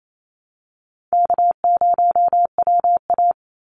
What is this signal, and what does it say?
Morse code, heard as a keyed tone, about 28 wpm, 700 Hz. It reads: K0WA
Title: Call Signs Repeated 3x - 28wpm